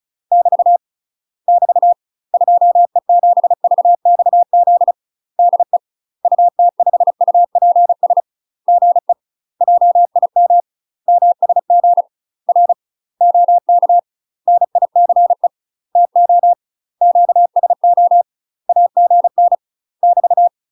Morse code, heard as a keyed tone, about 35 wpm, 700 Hz. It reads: <BT> <BT> 2E7VXZ DE UT5UPS GE JIM MSG R OK NICE TO QSO AGN <BT>
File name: UK MASTER QSOs TRAINING / 35WPM / UK MASTER QSOs 0038